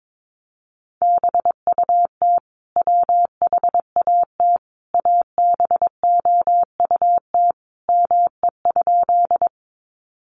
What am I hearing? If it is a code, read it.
Morse code, heard as a keyed tone, about 22 wpm, 700 Hz. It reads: BUT WHAT ABOUT ME?
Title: Sentences from Top 200 Words 22wpm